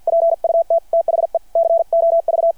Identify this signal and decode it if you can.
Morse code, heard as a keyed tone, about 51 wpm, 660 Hz. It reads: PUT THE COV
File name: archived / CRSnet